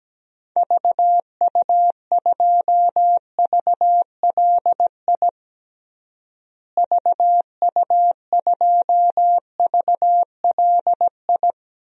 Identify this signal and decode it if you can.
Morse code, heard as a keyed tone, about 17 wpm, 700 Hz. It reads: VU2VLI VU2VLI
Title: DXCC-100 Call Signs - 6 Characters - Repeated 2x - 17wpm